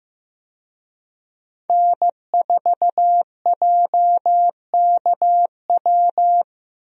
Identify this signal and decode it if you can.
Morse code, heard as a keyed tone, about 15 wpm, 700 Hz. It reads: N4JKW
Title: USA Call Signs - 5 Characters - Repeated 3x - 15wpm